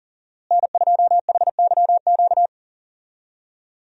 Morse code, heard as a keyed tone, about 40 wpm, 700 Hz. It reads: N2HYQ
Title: USA Call Signs - 5 Characters - Repeated 2x - 40wpm